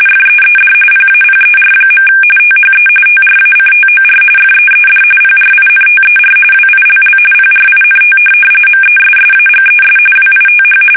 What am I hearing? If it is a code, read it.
RTTY, about 45.45 baud. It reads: FEED PAPER. THE TRACTOR FEED PRINTER WILL PRINT MESSAGES WITH
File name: fsk850 / teletype-corp